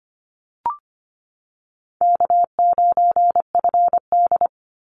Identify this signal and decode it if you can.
Morse code, heard as a keyed tone, about 25 wpm, 700 Hz. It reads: K9FD
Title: Call Signs Repeated 3x - 25wpm